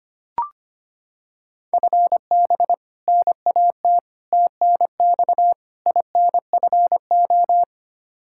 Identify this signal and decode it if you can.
Morse code, heard as a keyed tone, about 25 wpm, 700 Hz. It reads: FB NAT TNX INFO
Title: Practice QSO - Exchange 3 - More Info - 25wpm